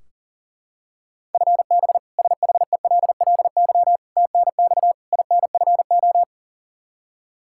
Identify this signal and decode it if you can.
Morse code, heard as a keyed tone, about 40 wpm, 700 Hz. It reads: FB SHELLY TNX INFO
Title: K1USN SST QSO Practice 40wpm